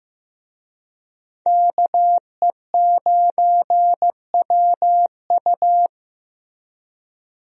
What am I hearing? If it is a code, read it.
Morse code, heard as a keyed tone, about 15 wpm, 700 Hz. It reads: KE9WU
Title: USA Call Signs - 5 Characters - Repeated 2x - 15wpm